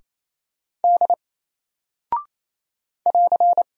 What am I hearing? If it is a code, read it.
Morse code, heard as a keyed tone, about 27 wpm, 700 Hz. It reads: D <AR>
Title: Single Character (Letter:Number:Punctuation:Prosign) 28wpm